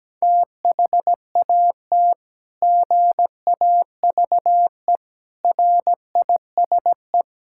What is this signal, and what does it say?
Morse code, heard as a keyed tone, about 17 wpm, 700 Hz. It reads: THAT GAVE RISE
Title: Sentences from Top 1000 Words - Review Part 6 - 17wpm